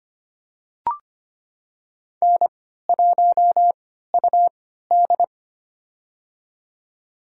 Morse code, heard as a keyed tone, about 25 wpm, spaced as for 14 wpm, 700 Hz. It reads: N1UD
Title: USA Call Signs - 4 Characters - 3x Character Spacing - 25wpm